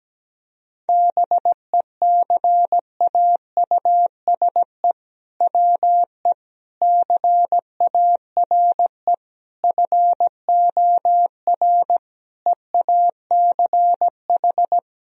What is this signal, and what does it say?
Morse code, heard as a keyed tone, about 17 wpm, 700 Hz. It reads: BECAUSE WE CARE FOR EACH
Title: Sentences from Top 1000 Words - Review Part 3 - 17wpm